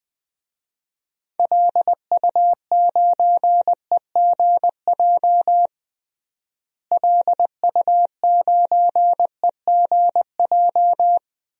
Morse code, heard as a keyed tone, about 21 wpm, 700 Hz. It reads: LU9EGJ LU9EGJ
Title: DXCC-100 Call Signs - 6 Characters - Repeated 2x - 20wpm